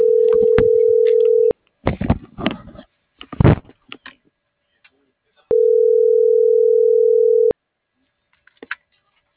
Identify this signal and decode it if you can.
Ringback tone (North America), 440 + 480 Hz.